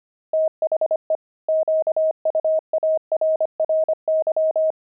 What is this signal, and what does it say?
Morse code, heard as a keyed tone, about 25 wpm, 630 Hz. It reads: THE QUARRY
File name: pangram-fs-25-cs-25-steaks